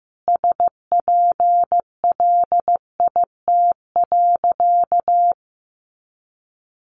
Morse code, heard as a keyed tone, about 15 wpm, 700 Hz. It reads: SPLIT.
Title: Easy Phrases 15wpm